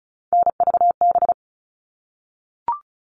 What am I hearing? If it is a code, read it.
Morse code, heard as a keyed tone, about 34 wpm, 700 Hz. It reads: NVB
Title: All Cut Numbers - 3-Digit 35wpm